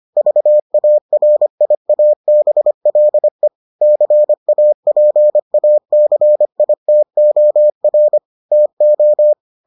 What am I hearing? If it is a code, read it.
Morse code, heard as a keyed tone, about 25 wpm, 600 Hz. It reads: VARIABLE CAPACITOR TO